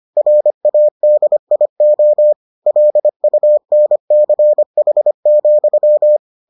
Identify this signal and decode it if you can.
Morse code, heard as a keyed tone, about 25 wpm, 600 Hz. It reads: RADIO LUNCH,